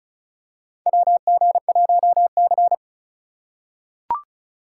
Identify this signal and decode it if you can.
Morse code, heard as a keyed tone, about 36 wpm, 700 Hz. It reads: WG1C
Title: USA Call Signs - 4 Characters - 6x Character Spacing - 35wpm